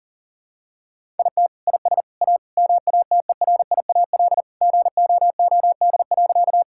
Morse code, heard as a keyed tone, about 39 wpm, 700 Hz. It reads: IT IS A MATERIAL GOOD.
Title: Sentences from Top 1000 Words - Review Part 5 - 40wpm